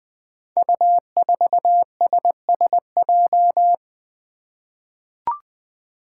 Morse code, heard as a keyed tone, about 20 wpm, 700 Hz. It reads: U4SSJ
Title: Five Challenging Letter-Number Combinations 20wpm